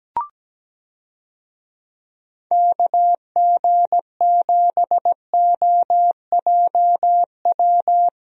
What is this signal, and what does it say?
Morse code, heard as a keyed tone, about 17 wpm, 700 Hz. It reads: KG7OJW
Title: USA Call Signs - 6 Characters - Repeated 2x - 17wpm